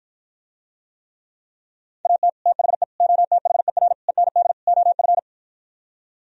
Morse code, heard as a keyed tone, about 52 wpm, 700 Hz. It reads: AT THE OTHER END OF